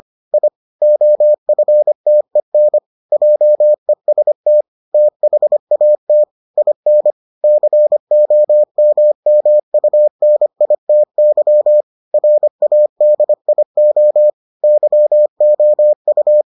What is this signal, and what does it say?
Morse code, heard as a keyed tone, about 25 wpm, 600 Hz. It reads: I OFTEN JEST THAT IN COMMUNITY RADIO YOU